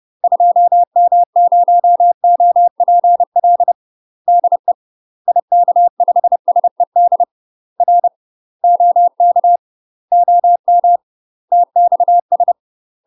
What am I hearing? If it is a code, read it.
Morse code, heard as a keyed tone, about 30 wpm, 700 Hz. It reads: 2M0OPL DE IK5SED R OK OM TXS